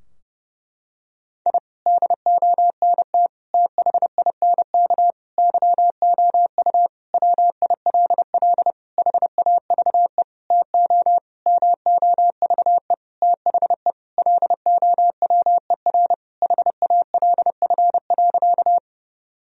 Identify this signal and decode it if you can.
Morse code, heard as a keyed tone, about 30 wpm, 700 Hz. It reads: I DONT THINK YOU WILL HAVE TO MOVE THE LOWER HALF.
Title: Sentences from Top 1000 Words - Review Part 4 - 30wpm